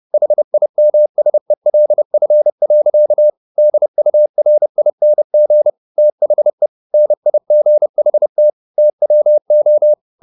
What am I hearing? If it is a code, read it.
Morse code, heard as a keyed tone, about 30 wpm, 600 Hz. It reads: HIMSELF. DURING THE NIGHT TWO
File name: the old man and the sea / the old man and the sea 1 30wpm